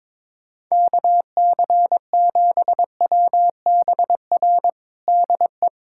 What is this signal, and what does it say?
Morse code, heard as a keyed tone, about 21 wpm, 700 Hz. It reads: KC7WBR DE
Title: USA Callsign DE Callsign - 6 Characters - 22wpm